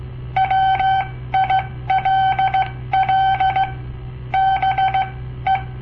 Morse code, heard as a keyed tone, about 17 wpm, spaced as for 14 wpm, 770 Hz. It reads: WILL BE
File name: papertapes / local / Code Practice-01